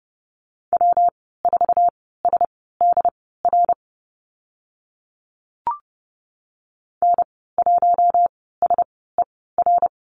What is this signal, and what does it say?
Morse code, heard as a keyed tone, about 30 wpm, spaced as for 17 wpm, 700 Hz. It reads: W4SDR N1SER
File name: Lesson 19g - Letter w - Call Signs 30wpm